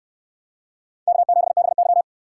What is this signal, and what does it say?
Morse code, heard as a keyed tone, about 64 wpm, 700 Hz. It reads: K7BY